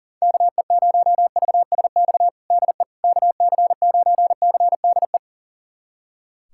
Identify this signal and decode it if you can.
Morse code, heard as a keyed tone, about 40 wpm, 700 Hz. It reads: KE0VSX DE KC9CDE
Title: USA Callsign DE Callsign 40wpm